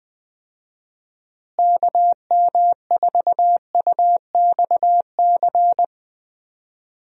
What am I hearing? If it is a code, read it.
Morse code, heard as a keyed tone, about 20 wpm, 700 Hz. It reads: KM4UXC
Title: USA Call Signs - 6 Characters - Repeated 2x - 20wpm